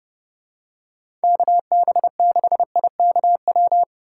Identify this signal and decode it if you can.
Morse code, heard as a keyed tone, about 30 wpm, 700 Hz. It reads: KB6IKW